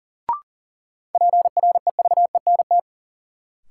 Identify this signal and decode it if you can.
Morse code, heard as a keyed tone, about 40 wpm, 700 Hz. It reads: PREVENT
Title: Sentences from Top 1100 Words - New Word Drill Alt - 40wpm